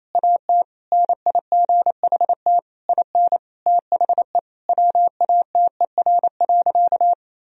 Morse code, heard as a keyed tone, about 28 wpm, 700 Hz. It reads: AT NIGHT IN THE WATER.
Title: Sentences from Top 1000 Words - Review Part 1 - 28wpm